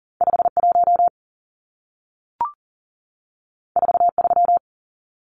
Morse code, heard as a keyed tone, about 40 wpm, 700 Hz. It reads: 51 43
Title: Numbers - 2-Digit 40wpm